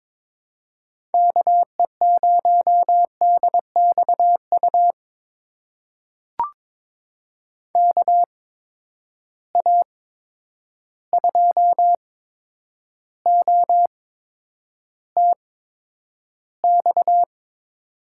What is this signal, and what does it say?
Morse code, heard as a keyed tone, about 22 wpm, spaced as for 9 wpm, 700 Hz. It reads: KE0DXU KA2OTX
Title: USA Call Signs - 6 Characters - 8x Character Spacing - 22wpm